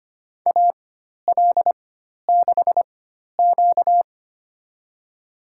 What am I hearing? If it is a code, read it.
Morse code, heard as a keyed tone, about 25 wpm, spaced as for 12 wpm, 700 Hz. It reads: AL6Q
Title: USA Call Signs - 4 Characters - 4x Character Spacing - 25wpm